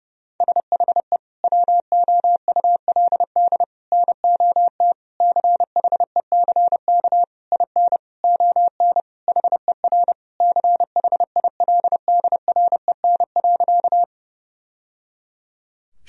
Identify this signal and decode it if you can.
Morse code, heard as a keyed tone, about 30 wpm, 700 Hz. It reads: SHE WOULD NOT CHECK IN ON HER CHILDREN.
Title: Sentences from Top 1000 Words - Review Part 3 - 30wpm